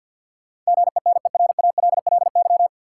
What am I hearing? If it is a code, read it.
Morse code, heard as a keyed tone, about 52 wpm, 700 Hz. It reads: GENERALLY
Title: Sentences from Top 600 Words - New Word Drill - 50wpm